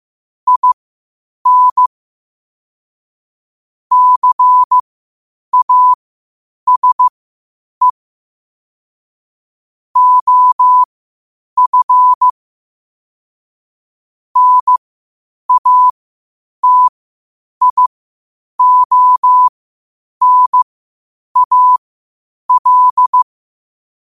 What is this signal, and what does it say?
Morse code, heard as a keyed tone, about 15 wpm, spaced as for 8 wpm, 1000 Hz. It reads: IN CASE OF NATIONAL
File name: qotdfiles / 8wpm